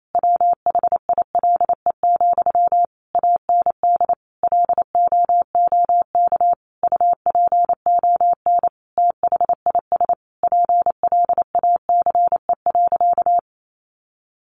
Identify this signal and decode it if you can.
Morse code, heard as a keyed tone, about 28 wpm, 700 Hz. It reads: WHILE, AND LOOK UPON THIS PLACE.
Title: Sentences from Top 1100 Words 28wpm